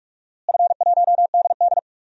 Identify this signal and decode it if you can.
Morse code, heard as a keyed tone, about 45 wpm, 700 Hz. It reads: F1DD
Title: DXCC-100 Call Signs - 4 Characters - Repeated 2x - 45wpm